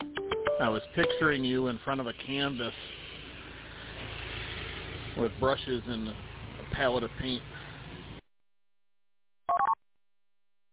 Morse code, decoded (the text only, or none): none